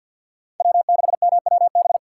Morse code, heard as a keyed tone, about 47 wpm, 700 Hz. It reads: W6MWB
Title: USA Call Signs - 5 Characters - 2x Character Spacing - 50wpm